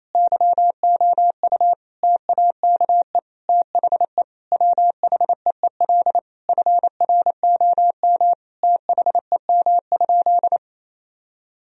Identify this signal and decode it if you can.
Morse code, heard as a keyed tone, about 27 wpm, 700 Hz. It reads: YOU TAKE THE WHEEL FROM THEM?
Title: Sentences from Top 1000 Words - Review Part 3 - 28wpm